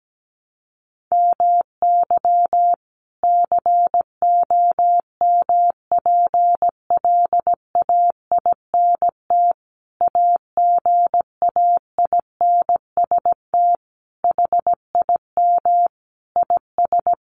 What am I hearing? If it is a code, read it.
Morse code, heard as a keyed tone, about 17 wpm, 700 Hz. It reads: MY COMPLAINT AGAINST HIM IS
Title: Sentences from Top 1400 Words 17wpm